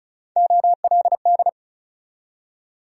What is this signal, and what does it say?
Morse code, heard as a keyed tone, about 46 wpm, 700 Hz. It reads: OLD